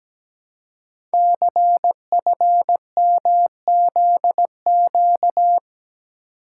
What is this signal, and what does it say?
Morse code, heard as a keyed tone, about 17 wpm, 700 Hz. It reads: CFMZQ
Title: Code Groups - 5 Letters - 17wpm